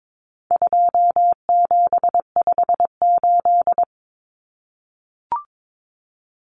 Morse code, heard as a keyed tone, about 22 wpm, 700 Hz. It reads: 2758
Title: Numbers - 4-Digit 22wpm